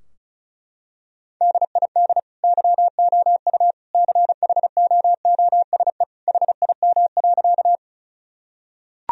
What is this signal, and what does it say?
Morse code, heard as a keyed tone, about 34 wpm, 700 Hz. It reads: DID YOU CHOOSE HIM.
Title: Sentences from Top 1000 Words - Review Part 4 - 35wpm